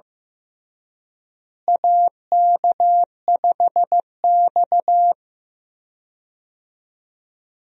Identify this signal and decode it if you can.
Morse code, heard as a keyed tone, about 15 wpm, 700 Hz. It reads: AK5X
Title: USA Call Signs - 4 Characters - Repeated 2x - 15wpm